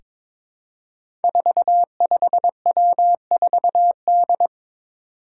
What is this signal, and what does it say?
Morse code, heard as a keyed tone, about 22 wpm, 700 Hz. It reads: 45W4D
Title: Five Challenging Letter-Number Combinations 22wpm